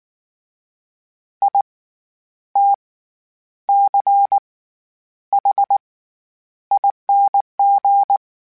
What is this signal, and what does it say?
Morse code, heard as a keyed tone, about 19 wpm, spaced as for 9 wpm, 800 Hz. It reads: ITCHING
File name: Special INGs / Special INGs-mp3s / SlowToFaster ing @18